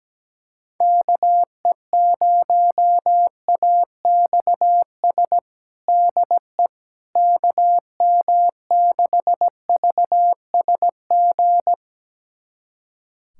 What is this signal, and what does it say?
Morse code, heard as a keyed tone, about 17 wpm, 700 Hz. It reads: KE0AXS DE KM6VSG
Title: USA Callsign DE Callsign 17wpm